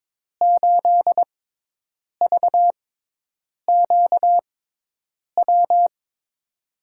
Morse code, heard as a keyed tone, about 22 wpm, spaced as for 8 wpm, 700 Hz. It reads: 8VQW
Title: USA Call Signs - 6 Characters - 6x Character Spacing - 22wpm